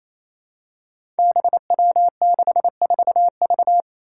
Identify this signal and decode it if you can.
Morse code, heard as a keyed tone, about 27 wpm, 700 Hz. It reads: BW64V
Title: Five Challenging Letter-Number Combinations 28wpm